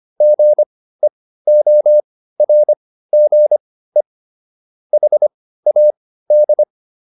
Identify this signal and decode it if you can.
Morse code, heard as a keyed tone, about 25 wpm, spaced as for 15 wpm, 600 Hz. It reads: GEORGE HAD